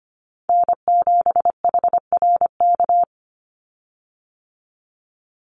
Morse code, heard as a keyed tone, about 25 wpm, 700 Hz. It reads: N7HRK
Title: Call Signs Repeated 2x - 25wpm